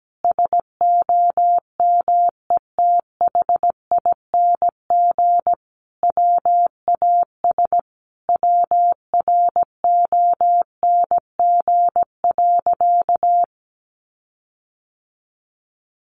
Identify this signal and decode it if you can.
Morse code, heard as a keyed tone, about 17 wpm, 700 Hz. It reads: SOMETHING WAS WRONG.
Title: Sentences from Top 1300 Words 17wpm